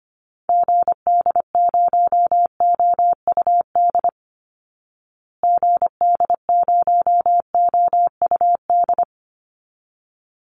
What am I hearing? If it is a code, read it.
Morse code, heard as a keyed tone, about 25 wpm, 700 Hz. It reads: GD0OUD GD0OUD
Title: Call Signs Repeated 3x - 25wpm